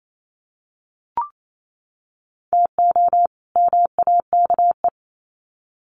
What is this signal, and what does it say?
Morse code, heard as a keyed tone, about 28 wpm, 700 Hz. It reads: TO MAKE
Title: Sets of 2 Words 28wpm